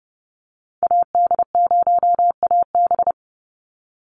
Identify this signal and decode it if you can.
Morse code, heard as a keyed tone, about 30 wpm, 700 Hz. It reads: AD0AB
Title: Call Signs - 30wpm